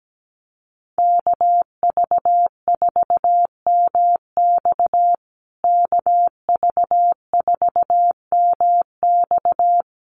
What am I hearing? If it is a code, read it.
Morse code, heard as a keyed tone, about 17 wpm, 700 Hz. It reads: KV4MX KV4MX
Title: USA Call Signs Repeated 3x 17wpm